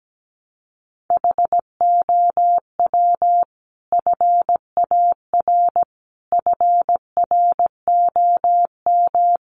Morse code, heard as a keyed tone, about 17 wpm, 700 Hz. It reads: HOW FAR FROM
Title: Sentences from Top 400 Words 17wpm